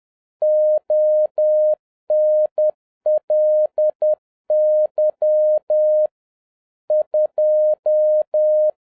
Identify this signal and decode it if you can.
Morse code, heard as a keyed tone, about 10 wpm, 610 Hz. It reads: ONLY 2